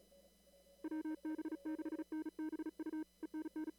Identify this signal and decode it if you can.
Morse code, heard as a keyed tone, about 36 wpm, 310 Hz. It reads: WB6NBU ENN
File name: CW / kt5j wpx cw-op-k5pi